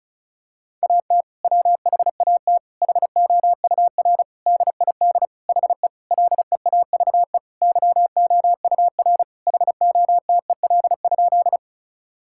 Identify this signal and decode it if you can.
Morse code, heard as a keyed tone, about 34 wpm, 700 Hz. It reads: AT WHAT HOUR DID HE LEAVE YOUR HOTEL?
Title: Sentences from Top 1600 Words 35wpm